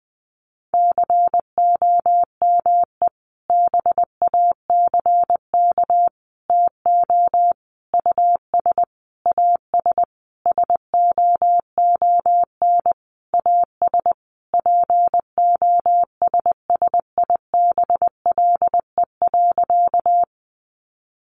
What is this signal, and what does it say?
Morse code, heard as a keyed tone, about 20 wpm, 700 Hz. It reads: COME BACK TO US AS SOON AS POSSIBLE.
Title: Sentences from Top 500 Words 20wpm